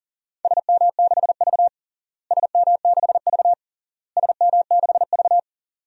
Morse code, heard as a keyed tone, about 40 wpm, 700 Hz. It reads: SM6V SM6V SM6V
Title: DXCC-100 Call Signs - 4 Characters - Repeated 3x - 40wpm